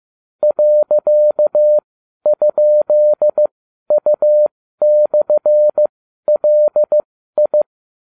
Morse code, heard as a keyed tone, about 15 wpm, spaced as for 12 wpm, 600 Hz. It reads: .?U/LI